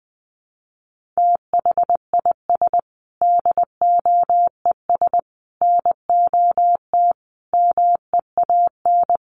Morse code, heard as a keyed tone, about 20 wpm, 700 Hz. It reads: THIS DOES NOT MEAN
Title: Sets of 5 Words Encore2 20wpm